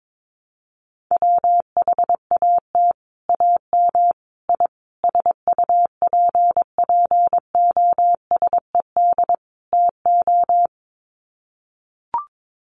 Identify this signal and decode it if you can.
Morse code, heard as a keyed tone, about 22 wpm, 700 Hz. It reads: WHAT AM I SUPPOSED TO
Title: Sets of 5 Words Encore2 22wpm